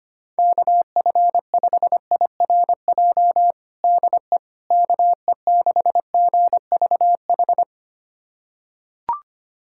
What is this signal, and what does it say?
Morse code, heard as a keyed tone, about 25 wpm, 700 Hz. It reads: KF5IRJ DE KE6GVH
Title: USA Callsign DE Callsign - 6 Characters - 25wpm